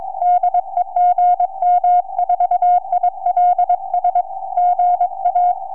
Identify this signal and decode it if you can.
Morse code, heard as a keyed tone, about 22 wpm, 710 Hz. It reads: DE GM4ILS GA
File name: gm4ils 4m